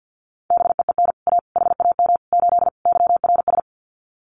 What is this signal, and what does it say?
Morse code, heard as a keyed tone, about 50 wpm, 700 Hz. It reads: BEEN A HAM 8 YRS
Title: Common QSO Phrases Encore 50wpm